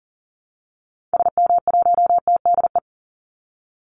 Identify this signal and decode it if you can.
Morse code, heard as a keyed tone, about 40 wpm, 700 Hz. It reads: SM1TDE
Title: Call Signs Repeated 2x - 40wpm